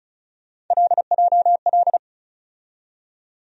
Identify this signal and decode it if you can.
Morse code, heard as a keyed tone, about 36 wpm, 700 Hz. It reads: LJL
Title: USA Call Sign Postfixes - 3 Letters 35wpm